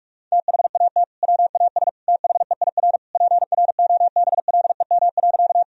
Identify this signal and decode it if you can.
Morse code, heard as a keyed tone, about 46 wpm, 700 Hz. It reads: THAT WAS THEIR PROBLEM.
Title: Sentences from Top 1000 Words - Review Part 2 - 45wpm